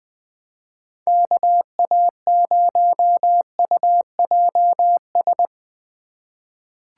Morse code, heard as a keyed tone, about 21 wpm, 700 Hz. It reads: KA0UJS
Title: USA Call Signs - 4x Character Spacing - 20wpm